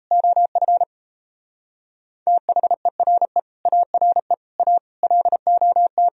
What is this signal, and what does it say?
Morse code, heard as a keyed tone, about 36 wpm, 700 Hz. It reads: OF THERE ARE A LOT